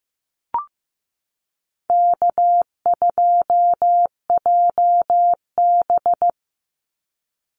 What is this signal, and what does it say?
Morse code, heard as a keyed tone, about 15 wpm, 700 Hz. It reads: K2JB
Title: NA SOTA Chaser Callsigns 15wpm